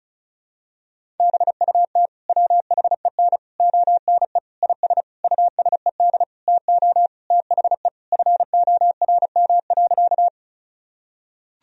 Morse code, heard as a keyed tone, about 36 wpm, 700 Hz. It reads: BUT WHEN ONE IS USED TO THE FORM.
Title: Sentences from Top 1000 Words - Review Part 1 - 35wpm